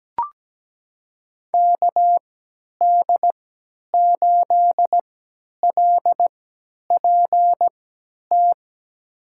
Morse code, heard as a keyed tone, about 17 wpm, spaced as for 10 wpm, 700 Hz. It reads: KD8LPT
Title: USA Call Signs - 6 Characters - 3x Character Spacing - 17wpm